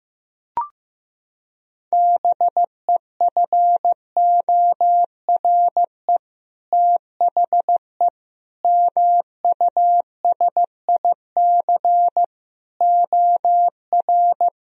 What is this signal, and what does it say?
Morse code, heard as a keyed tone, about 16 wpm, 700 Hz. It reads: BEFORE THE MUSIC OR
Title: Sentences from Top 1000 Words - Review Part 1 - 15wpm